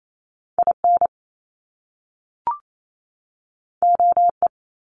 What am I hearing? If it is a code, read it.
Morse code, heard as a keyed tone, about 28 wpm, 700 Hz. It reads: I N O E
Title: Top Two Letter Combinations in Words 28wpm